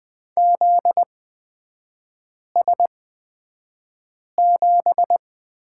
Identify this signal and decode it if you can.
Morse code, heard as a keyed tone, about 20 wpm, spaced as for 5 wpm, 700 Hz. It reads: ZS7
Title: 250 Lesson 40e - BK - Three Characters 20wpm